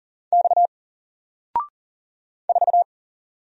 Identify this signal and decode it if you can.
Morse code, heard as a keyed tone, about 39 wpm, 700 Hz. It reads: X 4
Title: Single Letter-Number Alt 40wpm